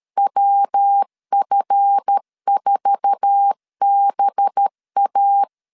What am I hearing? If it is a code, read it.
Morse code, heard as a keyed tone, about 13 wpm, 800 Hz. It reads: WF4BA
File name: LEZIONI / 8 YJ45 / 8 YJ4 12wpm 2151